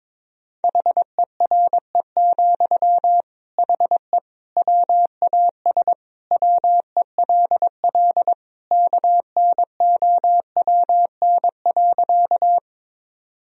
Morse code, heard as a keyed tone, about 21 wpm, 700 Hz. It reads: HERE, HE WAS WELL KNOWN.
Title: Sentences from Top 1000 Words - Review Part 3 - 22wpm